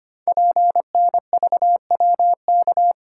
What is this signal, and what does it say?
Morse code, heard as a keyed tone, about 25 wpm, 700 Hz. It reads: PNVWK
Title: Code Groups - 5 Letters - 25wpm